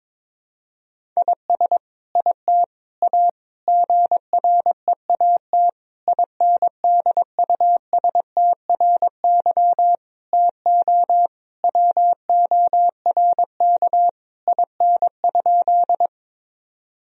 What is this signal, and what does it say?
Morse code, heard as a keyed tone, about 22 wpm, 700 Hz. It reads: IS IT A GREAT INDUSTRY TO WORK IN?
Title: Sentences from Top 1000 Words - Review Part 4 - 22wpm